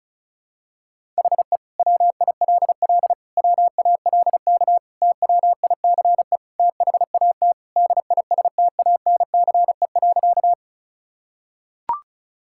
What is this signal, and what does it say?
Morse code, heard as a keyed tone, about 36 wpm, 700 Hz. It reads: HE WILL WALK TWICE THAT DISTANCE.
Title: Sentences from Top 1600 Words 35wpm